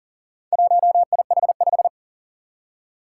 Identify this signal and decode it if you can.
Morse code, heard as a keyed tone, about 38 wpm, 700 Hz. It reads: 1IH5
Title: Four Challenging Letter-Number Combinations 40wpm